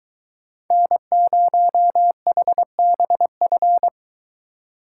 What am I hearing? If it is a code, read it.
Morse code, heard as a keyed tone, about 22 wpm, 700 Hz. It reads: N0HBF